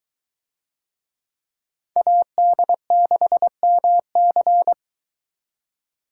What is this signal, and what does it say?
Morse code, heard as a keyed tone, about 22 wpm, 700 Hz. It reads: AD6MC